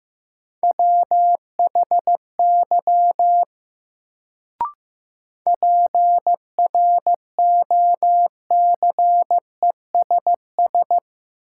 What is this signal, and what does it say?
Morse code, heard as a keyed tone, about 15 wpm, 700 Hz. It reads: WHY PROCESS
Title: Sets of 3 Words - New Word Drill - 15wpm